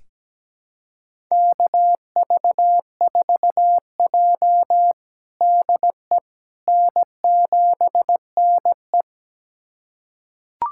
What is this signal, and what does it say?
Morse code, heard as a keyed tone, about 17 wpm, 700 Hz. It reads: KV4J DE N7NE
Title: USA Callsign DE Callsign - 4 Characters - 17wpm